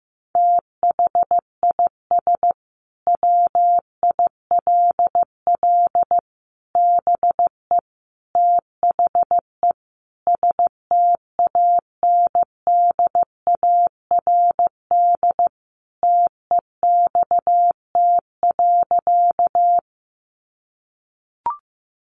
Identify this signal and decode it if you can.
Morse code, heard as a keyed tone, about 14 wpm, 700 Hz. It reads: THIS WILL BE THE STANDARD TEXT.
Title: Sentences from Top 1200 Words 15wpm